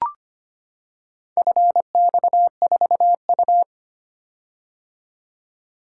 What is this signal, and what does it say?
Morse code, heard as a keyed tone, about 25 wpm, 700 Hz. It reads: FX4U
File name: practice / 25wpm / Four Letter-Number Combinations - 25wpm